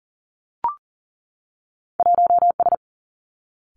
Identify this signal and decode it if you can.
Morse code, heard as a keyed tone, about 40 wpm, 700 Hz. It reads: 1 S
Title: Two Challenging Letter-Number Combinations 40wpm